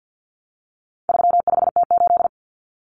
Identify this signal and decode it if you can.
Morse code, heard as a keyed tone, about 49 wpm, 700 Hz. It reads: 35T8
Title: Common Cut Numbers - 4-Digit 50wpm